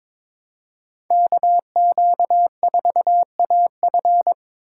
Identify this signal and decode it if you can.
Morse code, heard as a keyed tone, about 26 wpm, 700 Hz. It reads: KQ4AF
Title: Speed Racing - USA 2x2 Call Signs 22wpm